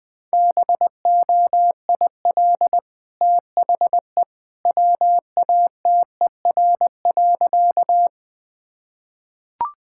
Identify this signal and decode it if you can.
Morse code, heard as a keyed tone, about 20 wpm, 700 Hz. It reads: BOIL THE WATER.
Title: Easy Sentences with Hints - 20wpm